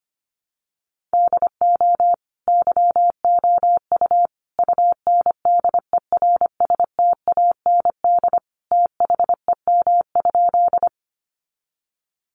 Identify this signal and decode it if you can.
Morse code, heard as a keyed tone, about 25 wpm, 700 Hz. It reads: DO YOU UNDERSTAND THEM?
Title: Sentences from Top 600 Words 25wpm